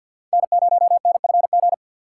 Morse code, heard as a keyed tone, about 46 wpm, 700 Hz. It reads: N0NFG
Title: USA Call Signs - 4x Character Spacing - 50wpm